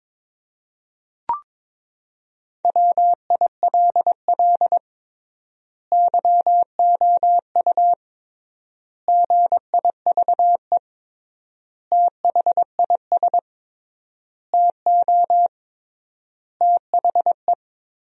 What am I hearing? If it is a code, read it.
Morse code, heard as a keyed tone, about 21 wpm, 700 Hz. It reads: WILL YOU GIVE THIS TO THE
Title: 3x Word-Spacing - Sentences from Top 200 Words - 22wpm